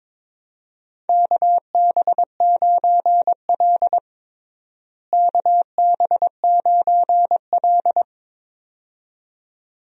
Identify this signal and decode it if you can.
Morse code, heard as a keyed tone, about 22 wpm, 700 Hz. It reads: KB9L KB9L
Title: USA Call Signs - 4 Characters - Repeated 2x - 22wpm